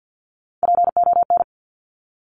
Morse code, heard as a keyed tone, about 50 wpm, 700 Hz. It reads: PON
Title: Top Three Letter Combinations in Words 50wpm